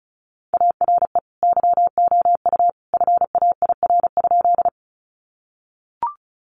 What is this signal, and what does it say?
Morse code, heard as a keyed tone, about 34 wpm, 700 Hz. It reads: ARE YOU FAIR?
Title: Sentences from Top 1300 Words 35wpm